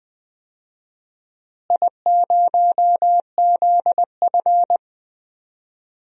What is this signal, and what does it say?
Morse code, heard as a keyed tone, about 20 wpm, 700 Hz. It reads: I0ZF
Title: DXCC-100 Call Signs - 4 Characters - Repeated 2x - 20wpm